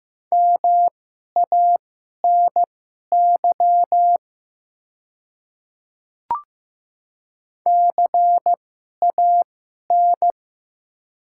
Morse code, heard as a keyed tone, about 15 wpm, spaced as for 11 wpm, 690 Hz. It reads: MANY CAN
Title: 2x Character Spacing - Top 200 Words - 15wpm